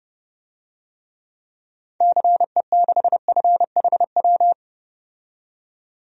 Morse code, heard as a keyed tone, about 30 wpm, 700 Hz. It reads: CE6FHW
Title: DXCC-100 Call Signs - 6 Characters - 30wpm